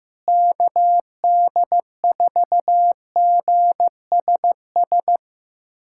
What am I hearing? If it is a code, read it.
Morse code, heard as a keyed tone, about 16 wpm, 700 Hz. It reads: KD4GSS
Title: USA Callsign DE Callsign - 6 Characters - 15wpm